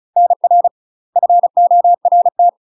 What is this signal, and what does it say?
Morse code, heard as a keyed tone, about 36 wpm, 700 Hz. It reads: NR FORT
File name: UK MASTER QSOs TRAINING / 35WPM / UK MASTER QSOs 0015